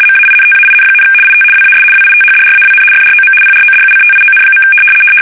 RTTY, about 75 baud. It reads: MAINTAIN, IT WAS WITHDRAWN FROM PRODUCTION.
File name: fsk850-75B / teletype-corp